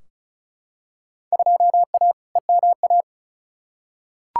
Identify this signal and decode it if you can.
Morse code, heard as a keyed tone, about 36 wpm, 700 Hz. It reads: 2A EMA
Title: Field Day Exchanges Encore - 35wpm